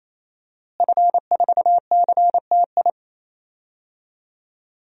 Morse code, heard as a keyed tone, about 28 wpm, 700 Hz. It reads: F4CTI
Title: DXCC-100 Call Signs - 5 Characters - Repeated 3x - 28wpm